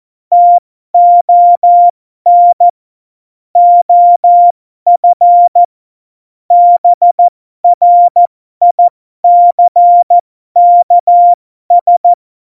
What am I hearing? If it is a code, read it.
Morse code, heard as a keyed tone, about 14 wpm, spaced as for 12 wpm, 700 Hz. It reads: TON OF BRICKS
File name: Morsekurs / Witze